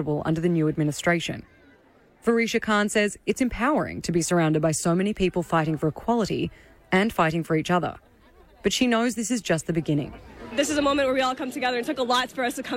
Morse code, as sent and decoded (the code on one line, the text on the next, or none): none